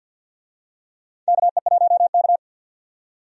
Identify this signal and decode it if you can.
Morse code, heard as a keyed tone, about 52 wpm, 700 Hz. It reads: KE1K
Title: USA Call Signs - 4 Characters - 50wpm